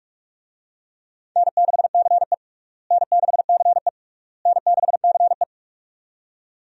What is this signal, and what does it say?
Morse code, heard as a keyed tone, about 47 wpm, 700 Hz. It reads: N6CE N6CE N6CE
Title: USA Call Signs - 4 Characters - Repeated 3x - 45wpm